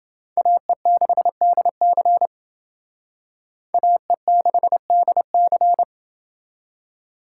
Morse code, heard as a keyed tone, about 29 wpm, 700 Hz. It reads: AE6DC AE6DC